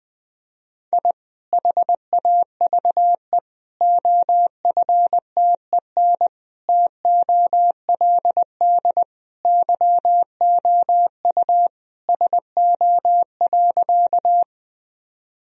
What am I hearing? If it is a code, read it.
Morse code, heard as a keyed tone, about 20 wpm, 700 Hz. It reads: I HAVE OFTEN TOLD YOU SO.
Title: Sentences from Top 1000 Words - Review Part 1 - 20wpm